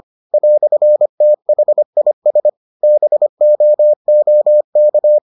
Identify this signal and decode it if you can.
Morse code, heard as a keyed tone, about 25 wpm, 600 Hz. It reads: "THIS BOOK